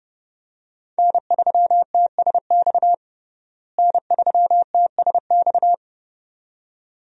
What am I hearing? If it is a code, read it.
Morse code, heard as a keyed tone, about 32 wpm, 700 Hz. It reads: N3TSX N3TSX
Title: USA Call Signs - 5 Characters - Repeated 2x - 30wpm